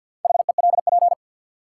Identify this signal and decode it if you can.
Morse code, heard as a keyed tone, about 53 wpm, 700 Hz. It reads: HELP
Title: Sentences from Top 200 Words - New Word Drill - 50wpm